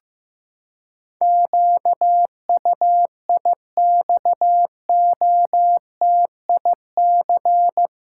Morse code, heard as a keyed tone, about 15 wpm, 700 Hz. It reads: QUIXOTIC
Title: Pangram Challenge v2 - 15wpm